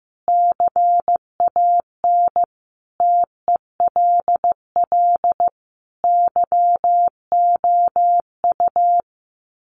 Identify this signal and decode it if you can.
Morse code, heard as a keyed tone, about 15 wpm, 700 Hz. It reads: CAN TELL YOU
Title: Sets of 5 Words Encore 15wpm